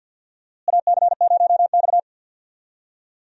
Morse code, heard as a keyed tone, about 52 wpm, 700 Hz. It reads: AC0X
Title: USA Call Signs - 4 Characters - Repeated 3x - 50wpm